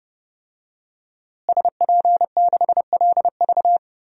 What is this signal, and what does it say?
Morse code, heard as a keyed tone, about 30 wpm, 700 Hz. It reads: SP6LV
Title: DXCC-100 Call Signs - 5 Characters - Repeated 2x - 30wpm